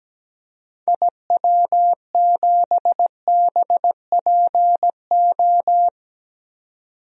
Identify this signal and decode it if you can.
Morse code, heard as a keyed tone, about 17 wpm, 700 Hz. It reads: IW7BPO
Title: DXCC-100 Call Signs - 6 Characters - 17wpm